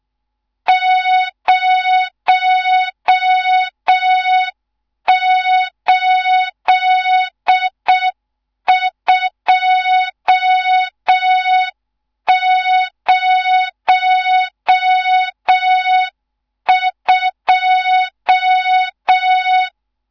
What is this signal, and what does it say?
Morse code, heard as a keyed tone, about 6 wpm, 3700 Hz. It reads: 08202
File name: Morze-HA2MN / hang / 14 280 HA2MN